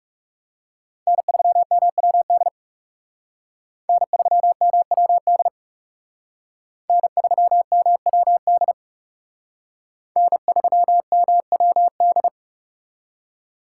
Morse code, heard as a keyed tone, about 37 wpm, 700 Hz. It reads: N3MWD N3MWD N3MWD N3MWD
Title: Speed Racing - USA 1x3 Call Signs 30wpm